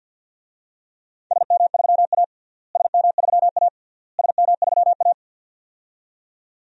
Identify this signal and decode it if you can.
Morse code, heard as a keyed tone, about 47 wpm, 700 Hz. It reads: SM3A SM3A SM3A
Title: DXCC-100 Call Signs - 4 Characters - Repeated 3x - 50wpm